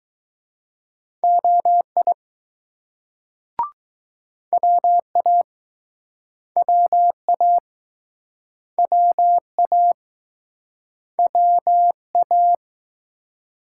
Morse code, heard as a keyed tone, about 19 wpm, 700 Hz. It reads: OI WA WA WA WA